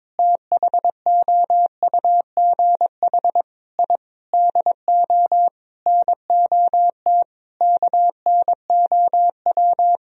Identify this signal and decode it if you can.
Morse code, heard as a keyed tone, about 22 wpm, 700 Hz. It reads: THOUGH I DO NOT KNOW
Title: Sentences from Top 1000 Words - Review Part 2 - 22wpm